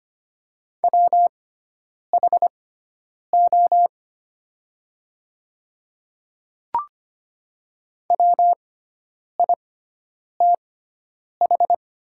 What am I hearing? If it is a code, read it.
Morse code, heard as a keyed tone, about 25 wpm, spaced as for 9 wpm, 690 Hz. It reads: WHO WITH
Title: 6x Character Spacing - Top 100 Words - 25wpm